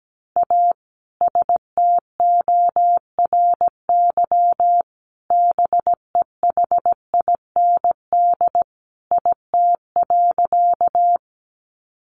Morse code, heard as a keyed tone, about 17 wpm, 700 Hz. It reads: A STORY BEHIND IT.
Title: Sentences from Top 500 Words 17wpm